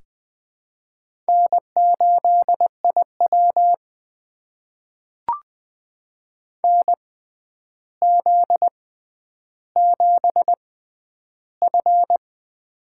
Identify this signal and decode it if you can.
Morse code, heard as a keyed tone, about 20 wpm, spaced as for 10 wpm, 700 Hz. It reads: N8IW NZ7F
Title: USA Call Signs - 4 Characters - 6x Character Spacing - 20wpm